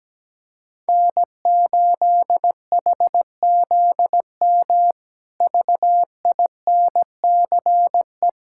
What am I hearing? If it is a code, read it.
Morse code, heard as a keyed tone, about 17 wpm, 700 Hz. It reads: N8HZM VINCE
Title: K1USN SST QSO Practice - 17wpm